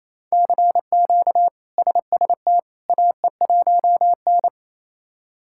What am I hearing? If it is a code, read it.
Morse code, heard as a keyed tone, about 28 wpm, 700 Hz. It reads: CQ SST AE1N
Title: K1USN SST QSO Practice - 28wpm